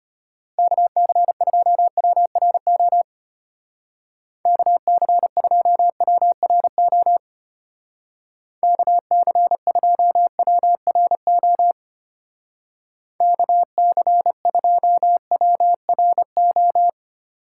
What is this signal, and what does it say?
Morse code, heard as a keyed tone, about 30 wpm, 700 Hz. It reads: KC2WRO KC2WRO KC2WRO KC2WRO
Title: Speed Racing - USA 2x3 Call Signs 25wpm